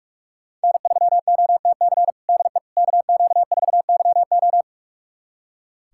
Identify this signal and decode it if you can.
Morse code, heard as a keyed tone, about 41 wpm, 700 Hz. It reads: N3OTC DE KQ4YO
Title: USA Callsign DE Callsign - 5 Characters - 45wpm